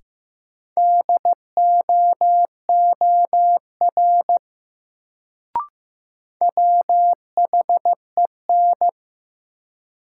Morse code, heard as a keyed tone, about 15 wpm, 700 Hz. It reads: DOOR WHEN
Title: Sets of 2 Words - New Word Drill - 15wpm